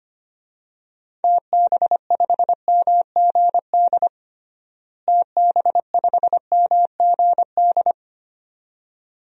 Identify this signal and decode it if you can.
Morse code, heard as a keyed tone, about 25 wpm, 700 Hz. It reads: TB5MGD TB5MGD
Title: DXCC-100 Call Signs - 6 Characters - Repeated 2x - 25wpm